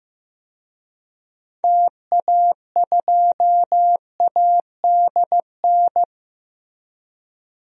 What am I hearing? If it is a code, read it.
Morse code, heard as a keyed tone, about 15 wpm, 700 Hz. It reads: TA2ADN
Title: DXCC-100 Call Signs - 6 Characters - 15wpm